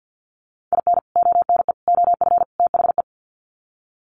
Morse code, heard as a keyed tone, about 48 wpm, 700 Hz. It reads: IN ONE OF THE
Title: Sets of 4 Words 50wpm